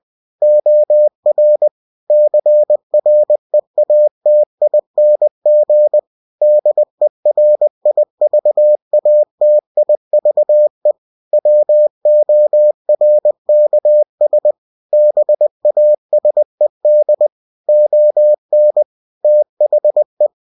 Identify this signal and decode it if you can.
Morse code, heard as a keyed tone, about 20 wpm, 600 Hz. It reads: OR CREATING DERIVATIVE WORKS BASED ON THE
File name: cw-books / winnie-the-pooh / winnie-20wpm-page-0114